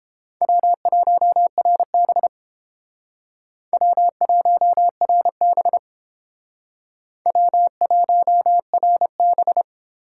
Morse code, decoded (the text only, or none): W1RB W1RB W1RB